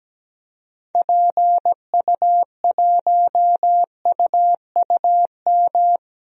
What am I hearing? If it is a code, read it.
Morse code, heard as a keyed tone, about 17 wpm, 700 Hz. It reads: PU1UUM
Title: DXCC-100 Call Signs - 6 Characters - Repeated 2x - 17wpm